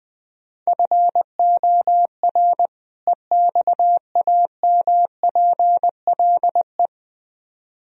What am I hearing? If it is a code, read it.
Morse code, heard as a keyed tone, about 20 wpm, 700 Hz. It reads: FOR EXAMPLE
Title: Speed-Racing - Sets of 2 Words 17wpm